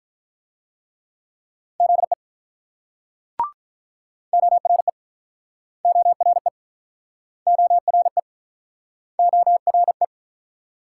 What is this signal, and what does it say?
Morse code, heard as a keyed tone, about 48 wpm, 700 Hz. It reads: ZE ORE ORE ORE ORE